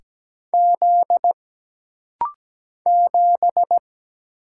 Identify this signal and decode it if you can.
Morse code, heard as a keyed tone, about 17 wpm, 700 Hz. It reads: Z 7
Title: Single Letter-Number - Rapid-Fire 17wpm